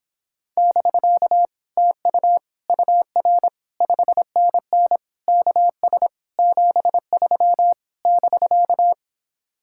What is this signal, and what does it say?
Morse code, heard as a keyed tone, about 26 wpm, 700 Hz. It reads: <BK> TU UR 5NN KS 73 <BK>